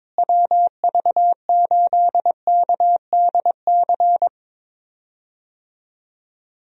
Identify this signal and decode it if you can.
Morse code, heard as a keyed tone, about 22 wpm, 700 Hz. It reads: WV8KDC
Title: USA Call Signs - 6 Characters - Repeated 2x - 22wpm